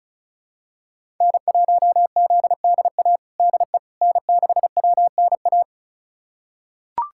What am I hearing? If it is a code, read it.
Morse code, heard as a keyed tone, about 36 wpm, 700 Hz. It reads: N1ZDA DE N6WNA
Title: USA Callsign DE Callsign - 5 Characters - 35wpm